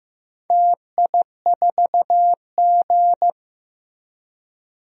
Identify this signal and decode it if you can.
Morse code, heard as a keyed tone, about 15 wpm, 700 Hz. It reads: TI4G
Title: DXCC-100 Call Signs - 4 Characters - Repeated 3x - 15wpm